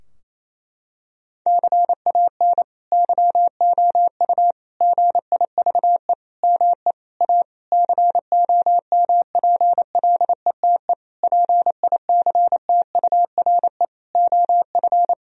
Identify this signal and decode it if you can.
Morse code, heard as a keyed tone, about 27 wpm, 700 Hz. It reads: CAN YOU GIVE ME A COMPLETE PICTURE OF
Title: Sentences from Top 1000 Words - Review Part 2 - 28wpm